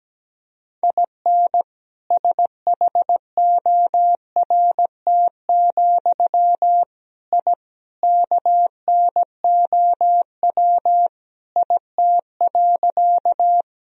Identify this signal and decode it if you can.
Morse code, heard as a keyed tone, about 17 wpm, 700 Hz. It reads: IN SHORT, I KNOW IT.
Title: Sentences from Top 1000 Words - Review Part 2 - 17wpm